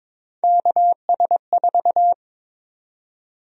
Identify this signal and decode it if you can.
Morse code, heard as a keyed tone, about 22 wpm, 700 Hz. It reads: KS4
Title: USA Call Sign Prefixes - Repeated 2x - 22wpm